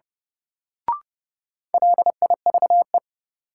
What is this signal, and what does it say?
Morse code, heard as a keyed tone, about 30 wpm, 700 Hz. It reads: LIVE
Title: Sentences from Top 200 Words - New Word Drill - 30wpm